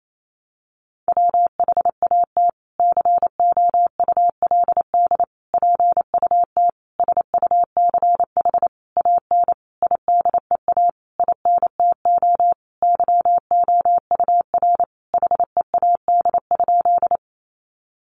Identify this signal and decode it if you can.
Morse code, heard as a keyed tone, about 28 wpm, 700 Hz. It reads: WHAT COULD PUT SUCH AN IDEA INTO YOUR HEAD?
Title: Sentences from Top 600 Words 28wpm